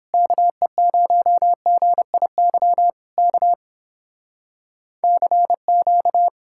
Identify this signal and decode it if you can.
Morse code, heard as a keyed tone, about 31 wpm, 700 Hz. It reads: KE0GIY K CQ